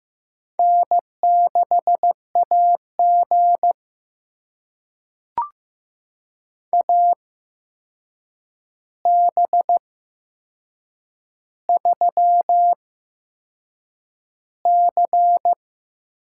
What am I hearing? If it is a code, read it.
Morse code, heard as a keyed tone, about 15 wpm, spaced as for 6 wpm, 700 Hz. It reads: N6AG AB3C
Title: USA Call Signs - 8x Character Spacing - 15wpm